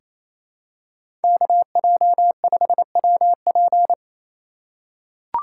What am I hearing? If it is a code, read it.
Morse code, heard as a keyed tone, about 28 wpm, 700 Hz. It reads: KJ5WP
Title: USA Call Signs - 5 Characters - 2x Character Spacing - 28wpm